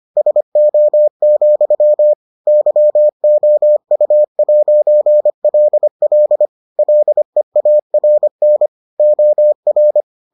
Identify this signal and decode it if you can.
Morse code, heard as a keyed tone, about 25 wpm, 600 Hz. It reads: SO, YOU'LL LEARN OR